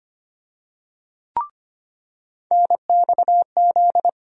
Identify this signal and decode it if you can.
Morse code, heard as a keyed tone, about 25 wpm, 700 Hz. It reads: NXZ
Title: USA Call Sign Postfixes - 3 Letters 25wpm